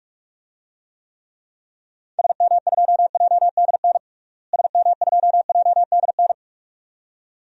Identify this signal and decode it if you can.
Morse code, heard as a keyed tone, about 41 wpm, 700 Hz. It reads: SM2JDN SM2JDN
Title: DXCC-100 Call Signs - 6 Characters - Repeated 2x - 45wpm